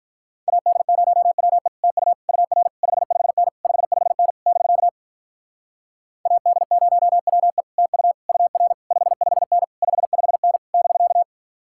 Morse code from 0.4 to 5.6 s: AD0WE TU UR 55N 55N <BK>

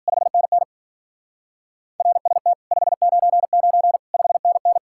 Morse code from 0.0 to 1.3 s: 5NN

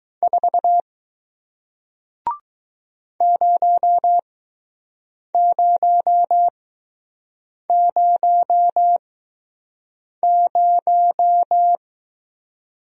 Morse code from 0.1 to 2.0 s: 4

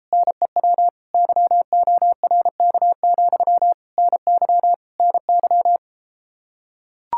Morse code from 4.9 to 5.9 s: NY

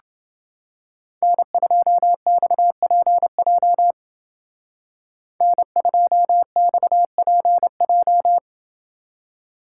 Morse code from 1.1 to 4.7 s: N2XPJ

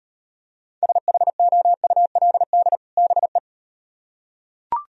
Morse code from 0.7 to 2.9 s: SHOULD

